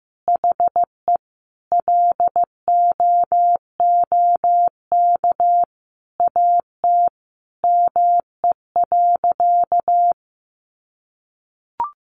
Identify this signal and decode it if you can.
Morse code, heard as a keyed tone, about 15 wpm, 700 Hz. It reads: HE LOOK AT ME.
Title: Sentences from Top 1500 Words 15wpm